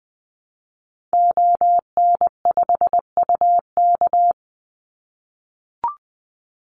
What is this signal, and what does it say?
Morse code, heard as a keyed tone, about 19 wpm, 700 Hz. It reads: ON5UK
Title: Call Signs - 20wpm